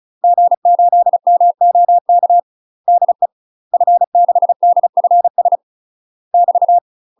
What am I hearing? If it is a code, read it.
Morse code, heard as a keyed tone, about 34 wpm, 700 Hz. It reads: G8MOK DE F6DFS <BT>
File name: UK MASTER QSOs TRAINING / 35WPM / UK MASTER QSOs 0034